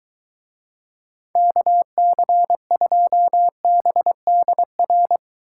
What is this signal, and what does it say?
Morse code, heard as a keyed tone, about 23 wpm, 700 Hz. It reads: KC2BDR